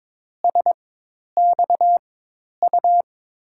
Morse code, heard as a keyed tone, about 22 wpm, spaced as for 10 wpm, 700 Hz. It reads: SXU
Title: USA Call Signs - 6 Characters - 4x Character Spacing - 22wpm